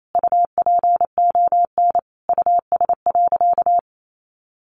Morse code, heard as a keyed tone, about 28 wpm, 700 Hz. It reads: UPON US.
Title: Sentences from Top 1400 Words 28wpm